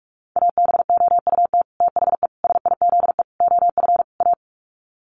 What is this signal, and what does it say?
Morse code, heard as a keyed tone, about 44 wpm, 700 Hz. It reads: ABOUT THE SIZE OF A
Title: Sets of 5 Words 45wpm